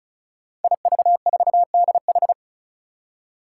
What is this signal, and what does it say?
Morse code, heard as a keyed tone, about 36 wpm, 700 Hz. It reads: IV4DH
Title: Five Challenging Letter-Number Combinations 35wpm